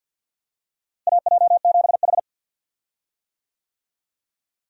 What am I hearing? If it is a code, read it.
Morse code, heard as a keyed tone, about 51 wpm, 700 Hz. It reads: AJ7H
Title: USA Call Signs - 4 Characters - Repeated 2x - 50wpm